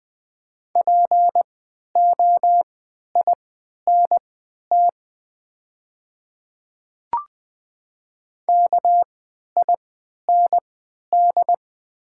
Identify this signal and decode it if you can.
Morse code, heard as a keyed tone, about 20 wpm, spaced as for 11 wpm, 700 Hz. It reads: POINT KIND
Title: 3x Character Spacing - Top 300 Words - 20wpm